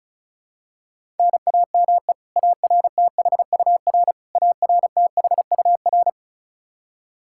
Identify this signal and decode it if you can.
Morse code, heard as a keyed tone, about 36 wpm, 700 Hz. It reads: NAME ARTHUR ARTHUR
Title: Practice QSO - Exchange 1 - Beginning - 35wpm